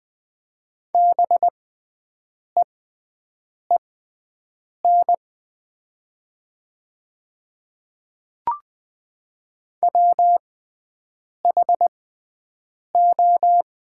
Morse code, heard as a keyed tone, about 20 wpm, spaced as for 7 wpm, 700 Hz. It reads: BEEN WHO
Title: 6x Character Spacing - Top 100 Words - 20wpm